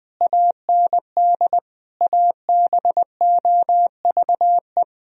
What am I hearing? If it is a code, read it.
Morse code, heard as a keyed tone, about 20 wpm, 700 Hz. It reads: AND ABOVE
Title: Sentences from Top 1000 Words - Review Part 2 - 20wpm